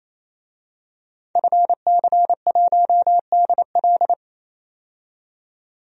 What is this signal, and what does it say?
Morse code, heard as a keyed tone, about 28 wpm, 700 Hz. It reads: FC1DL
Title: DXCC-100 Call Signs - 5 Characters - 28wpm